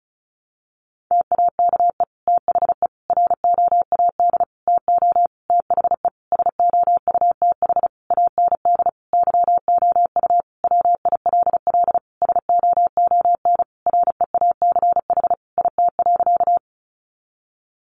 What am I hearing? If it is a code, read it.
Morse code, heard as a keyed tone, about 34 wpm, 700 Hz. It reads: TAKE THE ROAD TO THE SOUTH AND YOU WILL SOON REACH IT.
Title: Sentences from Top 400 Words 35wpm